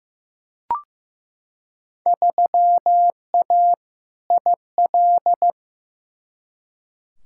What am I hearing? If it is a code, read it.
Morse code, heard as a keyed tone, about 15 wpm, 700 Hz. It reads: 3A IL
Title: Field Day Exchanges - 15wpm